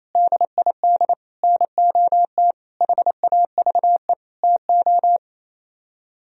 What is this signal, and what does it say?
Morse code, heard as a keyed tone, about 35 wpm, 700 Hz. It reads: DID NOT HAVE TO
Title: Speed-Racing - Sets of 5 Words 28wpm